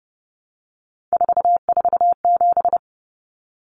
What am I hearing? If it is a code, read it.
Morse code, heard as a keyed tone, about 30 wpm, 700 Hz. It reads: 447
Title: Common Cut Numbers - 3-Digit 30wpm